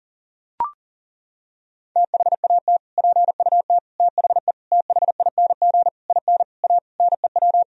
Morse code, heard as a keyed tone, about 40 wpm, 700 Hz. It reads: THAT PUT THE THING IN A NEW
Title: Sentences from Top 1000 Words - Review Part 1 - 40wpm